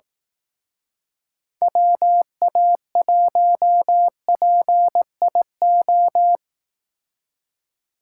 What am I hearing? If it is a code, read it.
Morse code, heard as a keyed tone, about 18 wpm, 700 Hz. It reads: WA1PIO
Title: Speed Racing - USA 2x3 Call Signs 15wpm